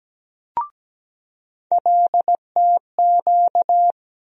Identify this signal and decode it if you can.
Morse code, heard as a keyed tone, about 17 wpm, 700 Hz. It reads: LTQ
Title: USA Call Sign Postfixes - 3 Letters 17wpm